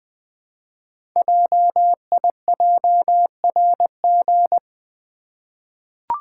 Morse code, heard as a keyed tone, about 20 wpm, 700 Hz. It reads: JIJRG
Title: Code Groups - 5 Letters - 20wpm